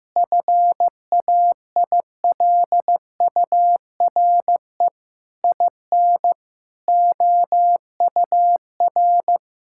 Morse code, heard as a keyed tone, about 16 wpm, 700 Hz. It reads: FAILURE IN OUR
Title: Sentences from Top 1000 Words - Review Part 6 - 15wpm